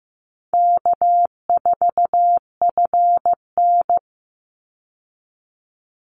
Morse code, heard as a keyed tone, about 15 wpm, 700 Hz. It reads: K4FN
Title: Easy Call Signs Repeated 3x - 15wpm